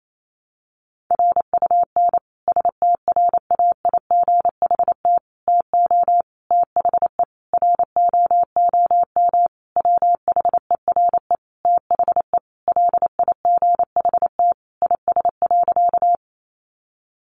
Morse code, heard as a keyed tone, about 28 wpm, 700 Hz. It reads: RUN STRAIGHT TO THE ROOM WHERE THE LIGHT IS.
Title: Sentences from Top 1000 Words 28wpm